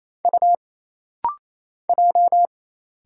Morse code, heard as a keyed tone, about 27 wpm, 700 Hz. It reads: U J
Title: Single Letters - Rapid-Fire 28wpm